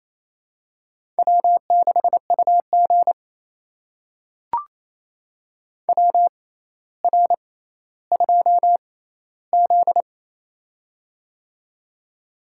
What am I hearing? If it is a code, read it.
Morse code, heard as a keyed tone, about 28 wpm, spaced as for 14 wpm, 700 Hz. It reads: W6UG WR2Z
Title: USA Call Signs - 4 Characters - 6x Character Spacing - 28wpm